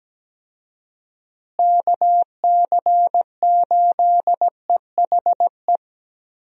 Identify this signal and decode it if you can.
Morse code, heard as a keyed tone, about 17 wpm, 700 Hz. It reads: KC8EHE
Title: USA Call Signs - 6 Characters - Repeated 3x - 17wpm